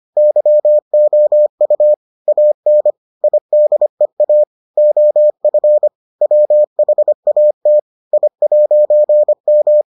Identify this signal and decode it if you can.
Morse code, heard as a keyed tone, about 25 wpm, 600 Hz. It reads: YOU AN IDEA OF WHAT I'M